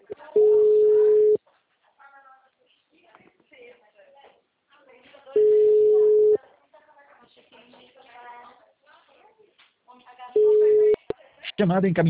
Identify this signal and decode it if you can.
Ringback tone (Europe, 425 Hz), with speech.
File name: out- -2024- - - .99300